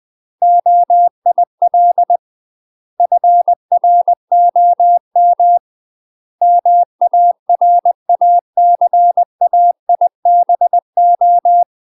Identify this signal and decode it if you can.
Morse code, heard as a keyed tone, about 20 wpm, 700 Hz. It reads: OIL FROM MARACAIBO
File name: UK MASTER QSOs TRAINING / 20WPM / UK MASTER QSOs 0032